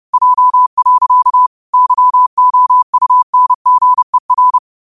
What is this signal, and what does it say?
Morse code, heard as a keyed tone, about 30 wpm, 1000 Hz. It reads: J. YOUNGER
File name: qotdfiles / 30wpm